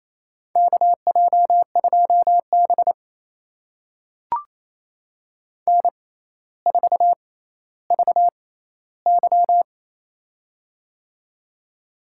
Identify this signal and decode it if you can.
Morse code, heard as a keyed tone, about 28 wpm, spaced as for 14 wpm, 700 Hz. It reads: KJ2B N4VY
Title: USA Call Signs - 4 Characters - 6x Character Spacing - 28wpm